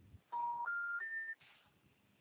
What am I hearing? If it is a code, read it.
Special information tone.
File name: out- -2016- - - .714